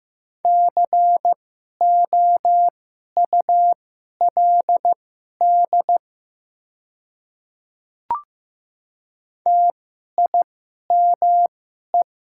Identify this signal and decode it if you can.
Morse code, heard as a keyed tone, about 15 wpm, spaced as for 11 wpm, 700 Hz. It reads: COULD TIME
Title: 2x Character Spacing - Top 100 Words - 15wpm